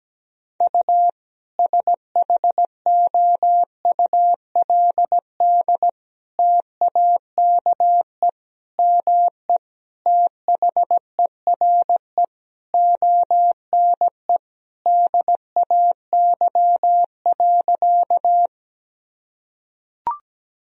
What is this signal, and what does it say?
Morse code, heard as a keyed tone, about 17 wpm, 700 Hz. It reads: U SHOULD TAKE ME THERE ONE DAY.
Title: Sentences from Top 300 Words and Abbreviations 17wpm